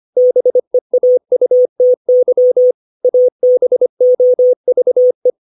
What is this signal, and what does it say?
Morse code, heard as a keyed tone, about 25 wpm, 500 Hz. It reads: BEAUTY ABOVE
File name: 632 words / 632 Words 25WPM